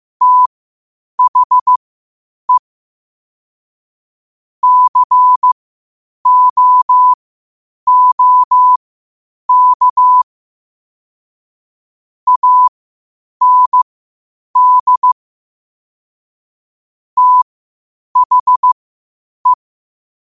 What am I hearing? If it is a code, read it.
Morse code, heard as a keyed tone, about 15 wpm, spaced as for 8 wpm, 1000 Hz. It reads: THE COOK AND THE
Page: Quotes for Sat, 16 Aug 2025 in Morse Code at 8 words per minute.